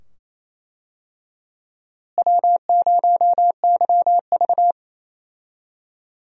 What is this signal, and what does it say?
Morse code, heard as a keyed tone, about 27 wpm, 700 Hz. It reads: W0YV
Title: USA Call Signs - 4 Characters - Repeated 2x - 28wpm